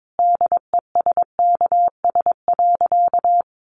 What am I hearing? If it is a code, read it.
Morse code, heard as a keyed tone, about 22 wpm, 700 Hz. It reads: DESKS.
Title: Additional Easy Phrases 22wpm